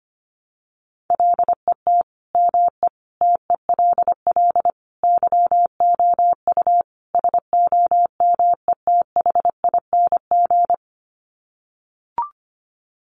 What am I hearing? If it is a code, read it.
Morse code, heard as a keyed tone, about 25 wpm, 700 Hz. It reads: LET ME TELL YOU SOMETHING
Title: Sets of 5 Words 25wpm